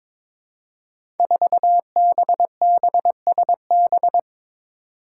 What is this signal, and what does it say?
Morse code, heard as a keyed tone, about 22 wpm, 700 Hz. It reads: 4BBSB
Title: Five Challenging Letter-Number Combinations 22wpm